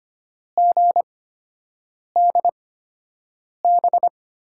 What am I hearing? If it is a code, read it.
Morse code, heard as a keyed tone, about 25 wpm, spaced as for 7 wpm, 700 Hz. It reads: GDB
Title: USA Call Signs - 6 Characters - 8x Character Spacing - 25wpm